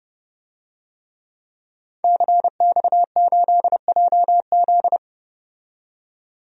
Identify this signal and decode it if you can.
Morse code, heard as a keyed tone, about 30 wpm, 700 Hz. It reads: CX8JZ
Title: DXCC-100 Call Signs - 5 Characters - 30wpm